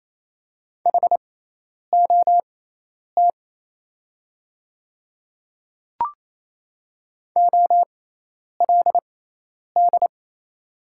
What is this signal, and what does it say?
Morse code, heard as a keyed tone, about 27 wpm, spaced as for 10 wpm, 700 Hz. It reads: HOT OLD
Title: 6x Character Spacing - Top 300 Words - 28wpm